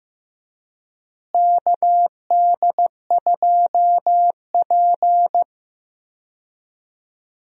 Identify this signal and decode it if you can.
Morse code, heard as a keyed tone, about 15 wpm, 700 Hz. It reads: KD2P
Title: USA Call Signs - 4 Characters - Repeated 3x - 15wpm